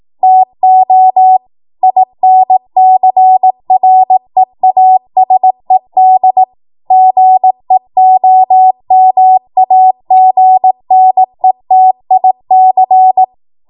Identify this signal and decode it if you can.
Morse code, heard as a keyed tone, about 18 wpm, 750 Hz. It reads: TO INCREASED GEOMAGNETIC
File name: Morse / 18WPM